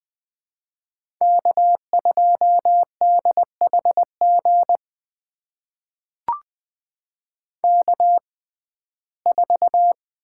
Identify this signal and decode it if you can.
Morse code, heard as a keyed tone, about 20 wpm, spaced as for 15 wpm, 700 Hz. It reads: K2DHG K4
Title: USA Call Signs - 5 Characters - 6x Character Spacing - 20wpm